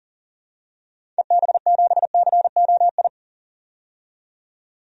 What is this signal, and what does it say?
Morse code, heard as a keyed tone, about 38 wpm, 700 Hz. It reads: EB7COI
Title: DXCC-100 Call Signs - 6 Characters - 40wpm